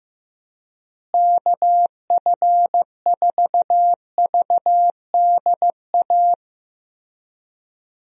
Morse code, heard as a keyed tone, about 15 wpm, 700 Hz. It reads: KF4VDA
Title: USA Call Signs - 6 Characters - Repeated 2x - 15wpm